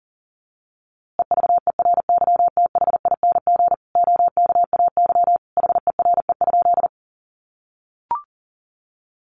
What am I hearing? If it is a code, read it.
Morse code, heard as a keyed tone, about 40 wpm, 700 Hz. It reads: EVERYTHING OKAY HERE?
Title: A Sound Of Thunder 40wpm